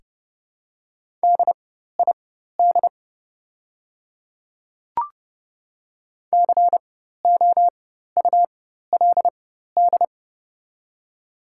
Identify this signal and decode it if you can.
Morse code, heard as a keyed tone, about 30 wpm, spaced as for 14 wpm, 690 Hz. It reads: DID COULD
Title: 4x Character Spacing - Top 100 Words - 30wpm